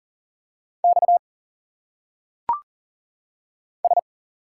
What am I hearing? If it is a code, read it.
Morse code, heard as a keyed tone, about 39 wpm, 700 Hz. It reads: X S
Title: Single Letters - Phonetic Pronunciation 40wpm